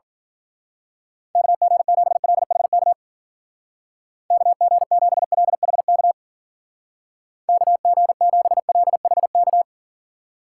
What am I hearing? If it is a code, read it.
Morse code, heard as a keyed tone, about 45 wpm, 700 Hz. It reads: KG7LHK KG7LHK KG7LHK
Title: Speed Racing - USA Call Signs 40wpm